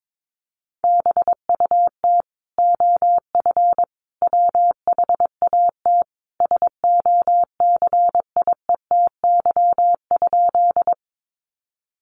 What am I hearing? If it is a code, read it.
Morse code, heard as a keyed tone, about 22 wpm, 700 Hz. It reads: BUT OF WHAT SOCIETY?
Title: Sentences from Top 700 Words 22wpm